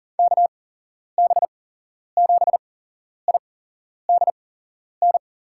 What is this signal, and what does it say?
Morse code, heard as a keyed tone, about 40 wpm, spaced as for 11 wpm, 700 Hz. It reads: KB7IDN
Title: USA Call Signs - 6 Characters - 8x Character Spacing - 40wpm